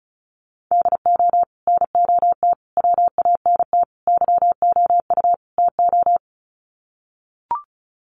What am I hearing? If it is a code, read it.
Morse code, heard as a keyed tone, about 34 wpm, 700 Hz. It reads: DO NOT WANT YOU TO
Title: Sets of 5 Words 35wpm